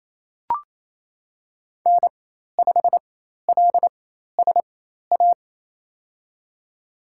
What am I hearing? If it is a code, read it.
Morse code, heard as a keyed tone, about 27 wpm, spaced as for 13 wpm, 700 Hz. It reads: N5LSA
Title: USA Call Signs - 5 Characters - 4x Character Spacing - 28wpm